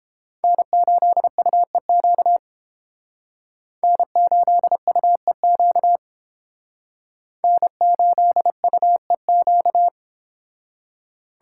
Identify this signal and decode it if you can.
Morse code, heard as a keyed tone, about 30 wpm, 700 Hz. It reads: N8UEQ N8UEQ N8UEQ